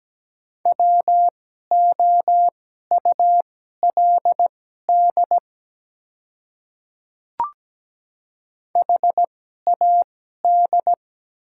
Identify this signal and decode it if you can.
Morse code, heard as a keyed tone, about 17 wpm, spaced as for 12 wpm, 700 Hz. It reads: WOULD HAD
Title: 2x Character Spacing - Top 100 Words - 17wpm